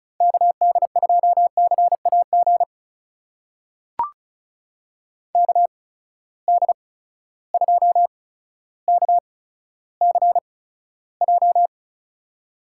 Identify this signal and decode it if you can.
Morse code, heard as a keyed tone, about 36 wpm, spaced as for 15 wpm, 700 Hz. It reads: KD2CAG KD2KCJ
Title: USA Call Signs - 6 Characters - 8x Character Spacing - 35wpm